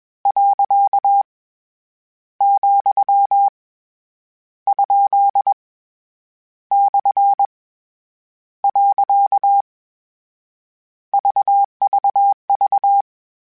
Morse code, heard as a keyed tone, about 21 wpm, 800 Hz. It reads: . , ? / . VVV
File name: Misc / Misc-mp3s / Punct @20